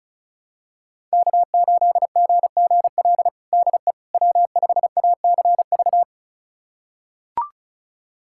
Morse code, heard as a keyed tone, about 34 wpm, 700 Hz. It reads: K8GGL DE W5ACV
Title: USA Callsign DE Callsign - 5 Characters - 35wpm